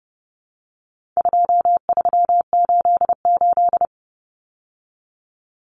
Morse code, heard as a keyed tone, about 30 wpm, 700 Hz. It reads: 2388
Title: Numbers - 4-Digit 30wpm